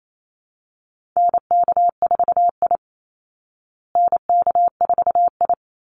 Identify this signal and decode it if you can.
Morse code, heard as a keyed tone, about 28 wpm, 700 Hz. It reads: NK4I NK4I
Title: Call Signs Repeated 3x - 28wpm